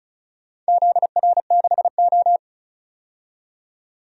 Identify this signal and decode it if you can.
Morse code, heard as a keyed tone, about 34 wpm, 700 Hz. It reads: ZR6O
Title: DXCC-100 Call Signs - 4 Characters - 35wpm